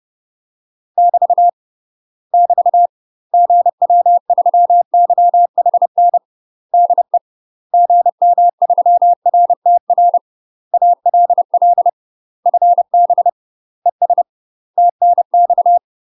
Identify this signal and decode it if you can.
Morse code, heard as a keyed tone, about 30 wpm, 700 Hz. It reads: <BT> <BT> GW3YHN DE GM3RTR ALL FB ES TNX
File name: UK MASTER QSOs TRAINING / 30WPM / UK MASTER QSOs 0051